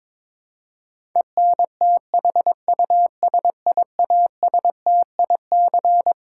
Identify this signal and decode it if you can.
Morse code, heard as a keyed tone, about 21 wpm, 700 Hz. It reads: ENTHUSIASTIC
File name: Twelve Letter Words - 22wpm